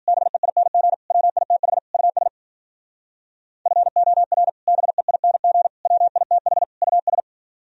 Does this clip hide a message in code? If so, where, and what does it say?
Yes: Morse code from 0.1 to 7.2 s, heard as a keyed tone, about 52 wpm, 700 Hz. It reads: BEING WITH US FOR BEING WITH US